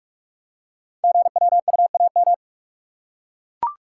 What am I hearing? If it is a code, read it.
Morse code, heard as a keyed tone, about 44 wpm, 700 Hz. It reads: GWUAM
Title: Code Groups - 5 Letters - 45wpm